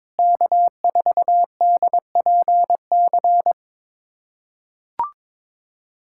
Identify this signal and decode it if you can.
Morse code, heard as a keyed tone, about 21 wpm, 700 Hz. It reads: K4DPC
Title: USA Callsign DE Callsign - 5 Characters - 22wpm